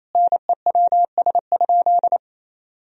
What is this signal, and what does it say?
Morse code, heard as a keyed tone, about 28 wpm, 700 Hz. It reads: NEWS?
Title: Sentences from Top 1000 Words - Review Part 4 - 28wpm